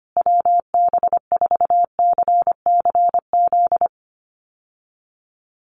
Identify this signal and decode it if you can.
Morse code, heard as a keyed tone, about 25 wpm, 700 Hz. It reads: WB4CCZ
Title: USA Call Signs Repeated 2x 25wpm